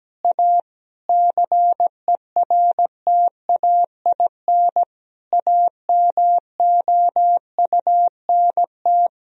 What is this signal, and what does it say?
Morse code, heard as a keyed tone, about 17 wpm, 700 Hz. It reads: A CERTAIN AMOUNT
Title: Sentences from Top 1000 Words - Review Part 4 - 17wpm